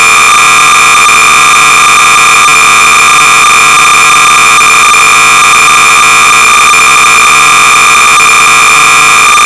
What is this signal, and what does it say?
Cassette data (Kansas City type).
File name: sorcerer / 300baud / Starbase Hyperion-(1980)-(Don Ursem)(Quality Software)-BASIC-Sorcerer-300baud